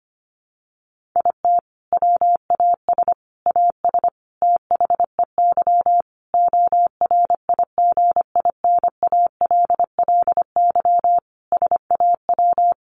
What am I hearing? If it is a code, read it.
Morse code, heard as a keyed tone, about 25 wpm, 700 Hz. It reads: IT WAS AS THEY ORIGINALLY SAW
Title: Sentences from Top 1500 Words 25wpm